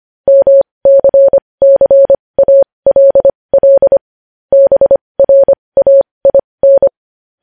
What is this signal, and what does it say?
Morse code, heard as a keyed tone, about 25 wpm, spaced as for 20 wpm, 550 Hz. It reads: MCCALL BRAIN
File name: podcast / News headlines-1x-25x20